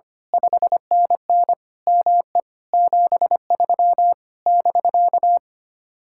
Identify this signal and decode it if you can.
Morse code, heard as a keyed tone, about 24 wpm, 700 Hz. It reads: 5NN ME 73 <BK>
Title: Speed-Racing Common QSO Phrases - 25wpm